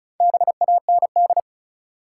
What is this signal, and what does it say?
Morse code, heard as a keyed tone, about 35 wpm, 700 Hz. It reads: BAND
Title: Four Letter Words - 35wpm